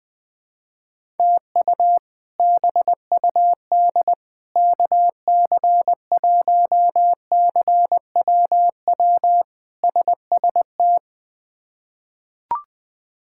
Morse code, heard as a keyed tone, about 20 wpm, 700 Hz. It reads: TU BUD KC1CWW SST
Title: K1USN SST QSO Practice - 20wpm